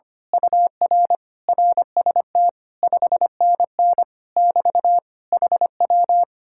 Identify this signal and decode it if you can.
Morse code, heard as a keyed tone, about 24 wpm, 700 Hz. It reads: UR RST 5NN <BT> HW
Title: Speed-Racing Common QSO Phrases Encore - 25wpm